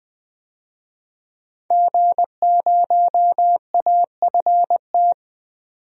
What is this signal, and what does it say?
Morse code, heard as a keyed tone, about 20 wpm, 700 Hz. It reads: G0AFT
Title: DXCC-100 Call Signs - 5 Characters - 20wpm